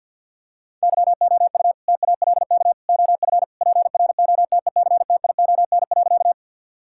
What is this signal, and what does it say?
Morse code, heard as a keyed tone, about 50 wpm, 700 Hz. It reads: YOU TALK OF PROTECTION.
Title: Sentences from Top 1600 Words 50wpm